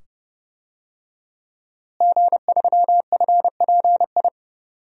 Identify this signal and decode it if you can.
Morse code, heard as a keyed tone, about 30 wpm, 700 Hz. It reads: G3FPI
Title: DXCC-100 Call Signs - 5 Characters - Repeated 2x - 30wpm